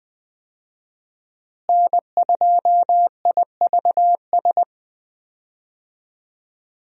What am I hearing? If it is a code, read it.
Morse code, heard as a keyed tone, about 20 wpm, 700 Hz. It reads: N2IVS
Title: USA Call Signs - 5 Characters - Repeated 2x - 20wpm